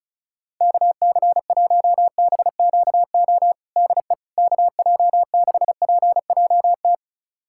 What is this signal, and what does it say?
Morse code, heard as a keyed tone, about 36 wpm, 700 Hz. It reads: KC1BQO DE KJ6PJT
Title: USA Callsign DE Callsign - 6 Characters - 35wpm